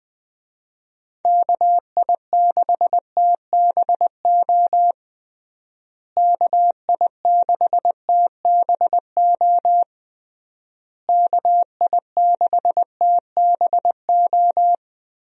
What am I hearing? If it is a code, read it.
Morse code, heard as a keyed tone, about 19 wpm, 700 Hz. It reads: KI6TBO KI6TBO KI6TBO
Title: USA Call Signs - 6 Characters - Repeated 3x - 20wpm